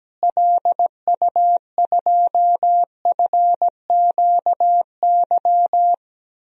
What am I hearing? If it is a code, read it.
Morse code, heard as a keyed tone, about 17 wpm, 700 Hz. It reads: LU2FQY
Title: DXCC-100 Call Signs - 6 Characters - Repeated 2x - 17wpm